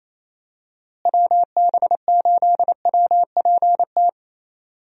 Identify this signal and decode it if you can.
Morse code, heard as a keyed tone, about 28 wpm, 700 Hz. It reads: WB8WPT
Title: USA Call Signs - 6 Characters - Repeated 3x - 28wpm